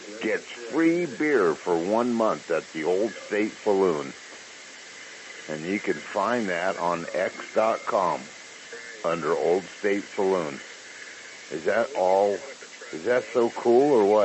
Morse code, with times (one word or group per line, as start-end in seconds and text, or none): none